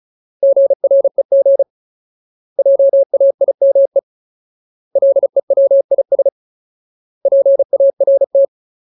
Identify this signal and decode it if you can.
Morse code, heard as a keyed tone, about 34 wpm, 550 Hz. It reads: GREG JAIME LEWIS PART
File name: Ultimate Word List / 35WPM Ultimate